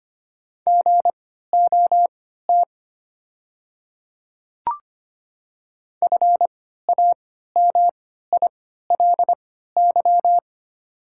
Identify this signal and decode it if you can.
Morse code, heard as a keyed tone, about 25 wpm, spaced as for 14 wpm, 690 Hz. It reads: GOT FAMILY
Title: 3x Character Spacing - Top 500 Words - 25wpm